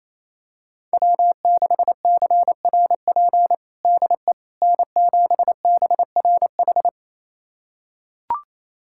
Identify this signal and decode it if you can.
Morse code, heard as a keyed tone, about 28 wpm, 700 Hz. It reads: W6CRP DE N7BRH
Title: USA Callsign DE Callsign - 5 Characters - 28wpm